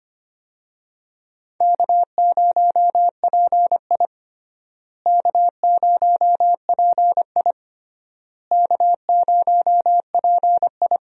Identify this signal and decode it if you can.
Morse code, heard as a keyed tone, about 25 wpm, 700 Hz. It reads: K0PI K0PI K0PI
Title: USA Call Signs - 4 Characters - Repeated 3x - 25wpm